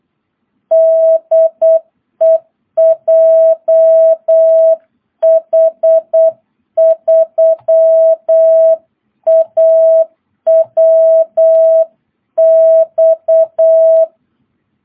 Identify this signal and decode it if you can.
Morse code, heard as a keyed tone, about 8 wpm, 660 Hz. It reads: DE JH3AWX
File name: jh8cbh / 2022cw